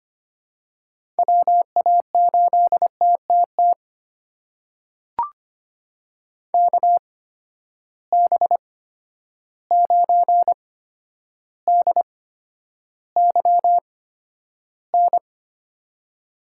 Morse code, heard as a keyed tone, about 25 wpm, spaced as for 11 wpm, 700 Hz. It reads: WA8TTT KB9DYN
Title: USA Call Signs - 6 Characters - 8x Character Spacing - 25wpm